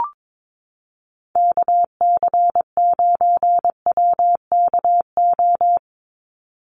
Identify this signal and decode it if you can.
Morse code, heard as a keyed tone, about 22 wpm, 700 Hz. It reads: KC9WKO
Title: USA Call Signs 22wpm